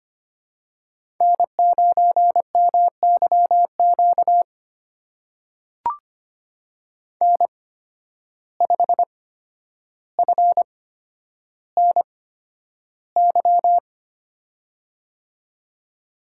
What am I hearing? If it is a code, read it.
Morse code, heard as a keyed tone, about 25 wpm, spaced as for 11 wpm, 700 Hz. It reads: N9MYQ N5FNY
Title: USA Call Signs - 5 Characters - 8x Character Spacing - 25wpm